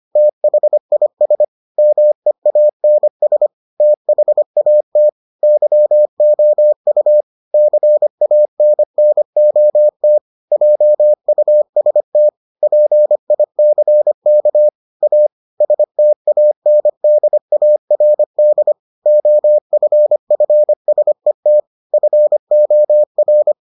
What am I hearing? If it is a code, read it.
Morse code, heard as a keyed tone, about 25 wpm, 600 Hz. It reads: THIS MEANS THAT YOU CANNOT JUST PICK A STANDARD OFFSET FOR